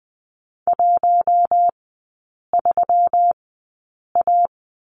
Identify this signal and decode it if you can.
Morse code, heard as a keyed tone, about 20 wpm, spaced as for 8 wpm, 700 Hz. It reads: 13A
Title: 121 Lesson 20e - Number 3 - Three Characters 20wpm